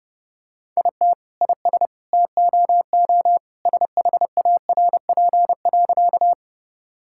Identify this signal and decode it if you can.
Morse code, heard as a keyed tone, about 30 wpm, 700 Hz. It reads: IT IS TOO SHARP.
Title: Sentences from Top 1600 Words 30wpm